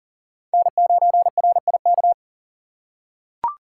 Morse code, heard as a keyed tone, about 41 wpm, 700 Hz. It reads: N9RIK
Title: USA Call Signs - 3x Character Spacing - 40wpm